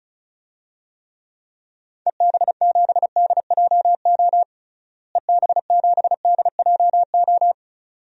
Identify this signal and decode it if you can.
Morse code, heard as a keyed tone, about 36 wpm, 700 Hz. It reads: EB7DJO EB7DJO
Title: DXCC-100 Call Signs - 6 Characters - Repeated 2x - 35wpm